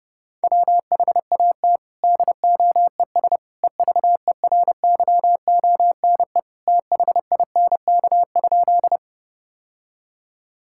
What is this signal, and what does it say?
Morse code, heard as a keyed tone, about 30 wpm, 700 Hz. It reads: WHAT DOES EVERYONE THINK?
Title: Sentences from Top 1000 Words - Review Part 5 - 30wpm